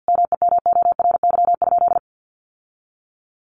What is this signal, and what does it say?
Morse code, heard as a keyed tone, about 50 wpm, 700 Hz. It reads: MEMORY?
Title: Sentences from Top 1300 Words 50wpm